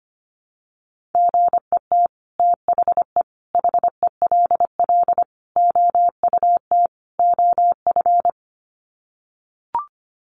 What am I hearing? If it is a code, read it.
Morse code, heard as a keyed tone, about 25 wpm, 700 Hz. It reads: GET THE HELL OUT OF
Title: Sets of 5 Words Encore2 25wpm